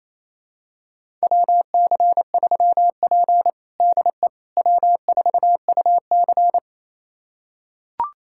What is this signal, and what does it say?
Morse code, heard as a keyed tone, about 28 wpm, 700 Hz. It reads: WC3P DE W4UC
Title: USA Callsign DE Callsign - 4 Characters - 28wpm